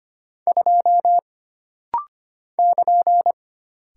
Morse code, heard as a keyed tone, about 25 wpm, 700 Hz. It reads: 2 <KN>
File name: Single Character (Letter:Number:Punctuation:Prosign) - Rapid-Fire - 25wpm